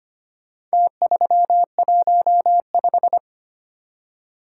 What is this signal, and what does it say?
Morse code, heard as a keyed tone, about 25 wpm, 700 Hz. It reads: T315
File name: practice / 25wpm / Four Letter-Number Combinations - 25wpm